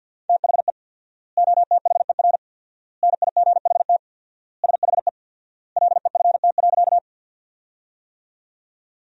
Morse code, heard as a keyed tone, about 48 wpm, 700 Hz. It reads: THE OTHER NIGHT SHE LEFT.
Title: 4x Word-Spacing - Sentences from Top 300 Words - 50wpm